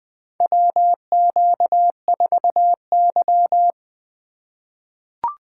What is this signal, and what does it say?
Morse code, heard as a keyed tone, about 20 wpm, 700 Hz. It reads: WQ4Y
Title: USA Call Signs - 4 Characters - 3x Character Spacing - 20wpm